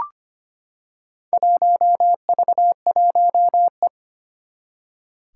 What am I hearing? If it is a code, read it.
Morse code, heard as a keyed tone, about 25 wpm, 700 Hz. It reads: 1V1E
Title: Four Challenging Letter-Number Combinations 25wpm